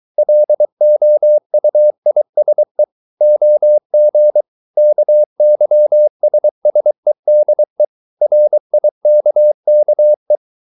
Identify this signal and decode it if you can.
Morse code, heard as a keyed tone, about 23 wpm, 600 Hz. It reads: LOUISE OG KYSSEDE RIKKE
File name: Ferie QRS 23wpm